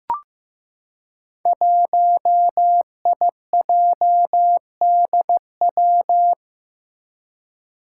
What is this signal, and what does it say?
Morse code, heard as a keyed tone, about 15 wpm, 700 Hz. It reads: 1IJDW
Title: Five Challenging Letter-Number Combinations 15wpm